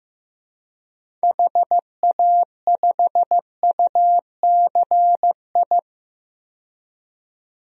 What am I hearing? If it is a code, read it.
Morse code, heard as a keyed tone, about 15 wpm, 700 Hz. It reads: HA5UCI
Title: DXCC-100 Call Signs - 6 Characters - 15wpm